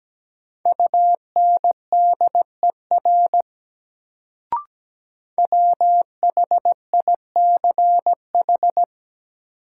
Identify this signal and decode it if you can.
Morse code, heard as a keyed tone, about 17 wpm, 700 Hz. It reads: UNDER WHICH
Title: Sets of 2 Words - New Word Drill - 17wpm